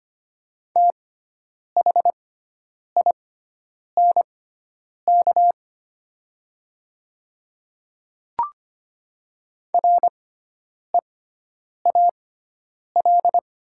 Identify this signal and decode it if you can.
Morse code, heard as a keyed tone, about 25 wpm, spaced as for 9 wpm, 700 Hz. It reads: THINK REAL
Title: 6x Character Spacing - Top 300 Words - 25wpm